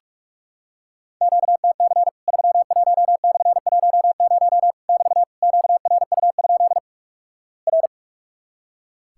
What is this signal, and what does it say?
Morse code, heard as a keyed tone, about 44 wpm, 700 Hz. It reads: QTC 31/10 <BT> QRU? R
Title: Worked All Europe DX Contest - QTC Exchange - 45wpm